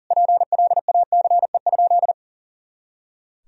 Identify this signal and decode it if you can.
Morse code, heard as a keyed tone, about 37 wpm, 700 Hz. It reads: PLACE?
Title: Sentences from Top 200 Words and Abbreviations 40wpm